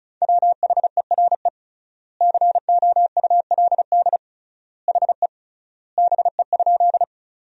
Morse code, heard as a keyed tone, about 34 wpm, 700 Hz. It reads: WHERE COULD HE BE?
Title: 3x Word-Spacing - Sentences from Top 200 Words - 35wpm